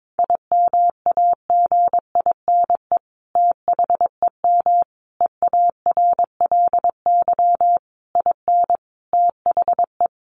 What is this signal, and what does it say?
Morse code, heard as a keyed tone, about 22 wpm, 700 Hz. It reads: IMAGINE THEM EARLY IN THE
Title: Sentences from Top 1400 Words 22wpm